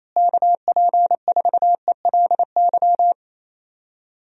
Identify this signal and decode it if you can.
Morse code, heard as a keyed tone, about 28 wpm, 700 Hz. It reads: KP4ELY
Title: USA Call Signs - 6 Characters - Repeated 3x - 28wpm